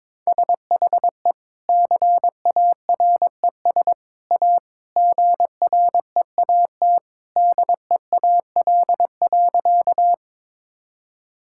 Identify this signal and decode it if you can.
Morse code, heard as a keyed tone, about 22 wpm, 700 Hz. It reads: SHE CARES A GREAT DEAL.
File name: practice / 22wpm / Sentences with 3-Word Phrases - 22wpm